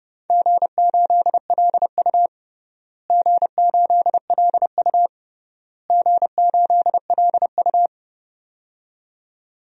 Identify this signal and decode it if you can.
Morse code, heard as a keyed tone, about 32 wpm, 700 Hz. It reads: G8LU G8LU G8LU
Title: DXCC-100 Call Signs - 4 Characters - Repeated 3x - 30wpm